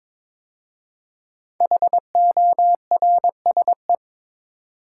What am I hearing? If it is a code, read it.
Morse code, heard as a keyed tone, about 22 wpm, 700 Hz. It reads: HORSE
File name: 5 Letter Words - First 3 Letters Given - 22wpm